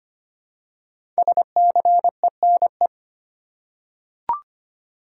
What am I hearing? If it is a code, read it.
Morse code, heard as a keyed tone, about 25 wpm, 700 Hz. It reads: SCENE
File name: 5 Letter Words - First 3 Letters Given - 25wpm